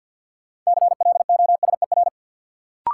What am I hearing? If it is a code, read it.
Morse code, heard as a keyed tone, about 51 wpm, 700 Hz. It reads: CLOSER
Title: Sets of 5 Words Encore - New Word Drill - 50wpm